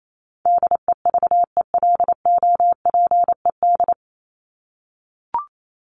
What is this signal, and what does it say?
Morse code, heard as a keyed tone, about 28 wpm, 700 Hz. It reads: DEVELOPED
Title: ED Suffix - Top 100 Words with Base Words 28wpm